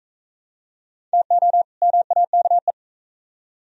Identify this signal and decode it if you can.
Morse code, heard as a keyed tone, about 44 wpm, 700 Hz. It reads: TO MAKE